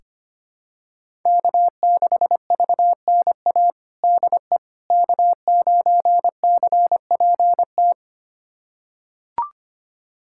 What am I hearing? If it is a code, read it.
Morse code, heard as a keyed tone, about 25 wpm, 700 Hz. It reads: K6VNA DE K9CPT
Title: USA Callsign DE Callsign - 5 Characters - 25wpm